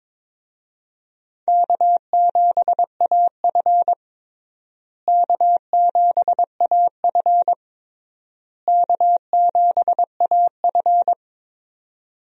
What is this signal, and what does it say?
Morse code, heard as a keyed tone, about 22 wpm, 700 Hz. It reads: K7AF K7AF K7AF
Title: USA Call Signs - 4 Characters - Repeated 3x - 22wpm